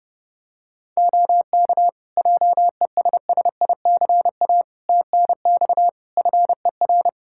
Morse code, heard as a keyed tone, about 31 wpm, 700 Hz. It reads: OK JESSICA TNX FER
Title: Practice QSO - Exchange 4 - Ending - 30wpm